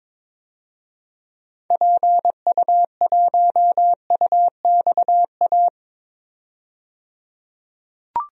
Morse code, heard as a keyed tone, about 22 wpm, 700 Hz. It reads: PU1UXA
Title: DXCC-100 Call Signs - 6 Characters - 22wpm